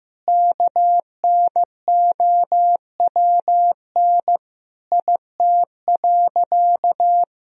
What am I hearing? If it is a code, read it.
Morse code, heard as a keyed tone, about 16 wpm, 700 Hz. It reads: KNOWN IT.
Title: Sentences from Top 1000 Words - Review Part 3 - 15wpm